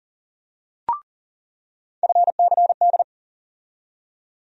Morse code, heard as a keyed tone, about 40 wpm, 700 Hz. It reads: FCD
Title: USA Call Sign Postfixes - 3 Letters 40wpm